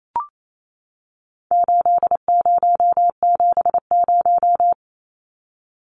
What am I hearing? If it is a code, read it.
Morse code, heard as a keyed tone, about 28 wpm, 700 Hz. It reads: 8070
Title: Numbers - 4-Digit 28wpm